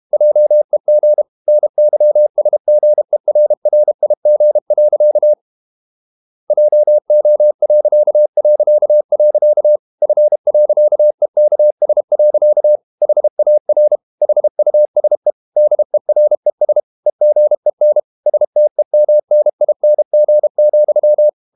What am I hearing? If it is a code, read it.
Morse code, heard as a keyed tone, about 32 wpm, 600 Hz. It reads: JEG NYSGERRIG. JO... F.EKS. HAR HUSE DERES EGEN STEMNING,
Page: Ferie 32 wpm